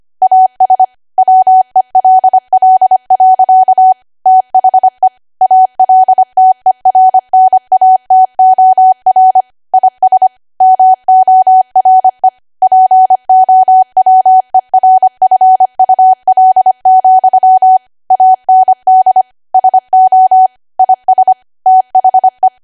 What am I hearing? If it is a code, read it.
Morse code, heard as a keyed tone, about 25 wpm, 750 Hz. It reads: AS WELL. THE ALTERNATOR IS MORE POWERFUL, AND SO IS THE
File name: Archive / 25 WPM / 25WPM